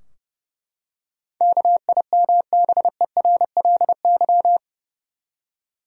Morse code, heard as a keyed tone, about 32 wpm, 700 Hz. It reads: KIMBERLY
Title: USA Common Names - 30wpm